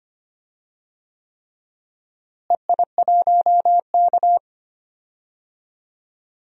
Morse code, heard as a keyed tone, about 25 wpm, 700 Hz. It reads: EI1K
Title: DXCC-100 Call Signs - 4 Characters - Repeated 3x - 25wpm